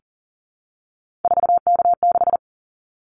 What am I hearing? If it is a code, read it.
Morse code, heard as a keyed tone, about 39 wpm, 700 Hz. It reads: 4K6
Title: Callsign Prefixes for Top 100 DXCC - 40wpm